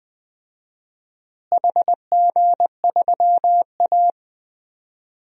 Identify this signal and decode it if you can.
Morse code, heard as a keyed tone, about 20 wpm, 700 Hz. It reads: HG3A
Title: DXCC-100 Call Signs - 4 Characters - Repeated 3x - 20wpm